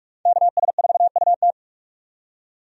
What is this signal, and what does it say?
Morse code, heard as a keyed tone, about 46 wpm, 700 Hz. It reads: KS4UT
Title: USA Call Signs - 5 Characters - 45wpm